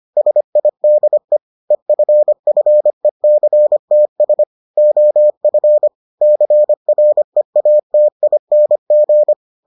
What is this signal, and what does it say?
Morse code, heard as a keyed tone, about 25 wpm, 600 Hz. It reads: SIDE EFFECTS OF CREATING